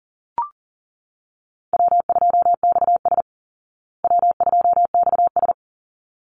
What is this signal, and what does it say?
Morse code, heard as a keyed tone, about 40 wpm, 700 Hz. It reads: W2XS W2XS
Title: Easy Call Signs Repeated 2x - 40wpm